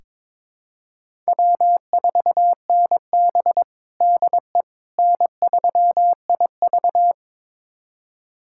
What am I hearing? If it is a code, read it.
Morse code, heard as a keyed tone, about 22 wpm, 700 Hz. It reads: W4NB DE N3IV
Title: USA Callsign DE Callsign - 4 Characters - 22wpm